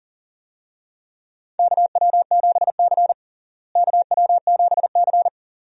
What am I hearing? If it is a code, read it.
Morse code, heard as a keyed tone, about 40 wpm, 700 Hz. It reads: KW7C KW7C
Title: USA Call Signs - 4 Characters - Repeated 2x - 40wpm